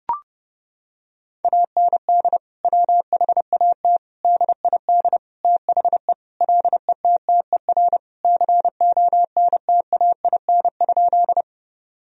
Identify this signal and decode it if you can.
Morse code, heard as a keyed tone, about 30 wpm, 700 Hz. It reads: AND WHAT DID THE LETTER CONTAIN?
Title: Sentences from Top 1000 Words - Review Part 2 - 30wpm